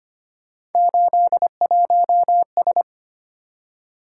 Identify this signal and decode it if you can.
Morse code, heard as a keyed tone, about 25 wpm, 700 Hz. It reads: 81S
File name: Three Letter-Number Combinations - 25wpm